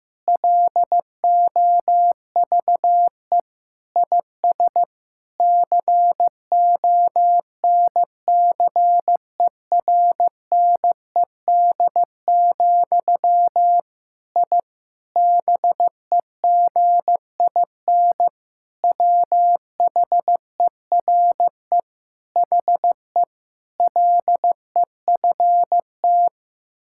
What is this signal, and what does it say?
Morse code, heard as a keyed tone, about 15 wpm, 700 Hz. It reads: LOVE IS CONCERNED, I BEGIN WHERE HE LEFT
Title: Sentences from Top 1000 Words - Review Part 6 - 15wpm